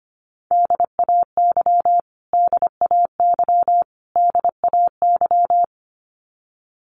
Morse code, heard as a keyed tone, about 25 wpm, 700 Hz. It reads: DAY DAY DAY
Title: Top 500 Words Repeated 25wpm